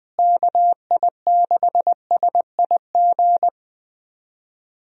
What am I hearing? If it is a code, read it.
Morse code, heard as a keyed tone, about 20 wpm, 700 Hz. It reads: KI6SIG
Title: USA Call Signs - 6 Characters - Repeated 3x - 20wpm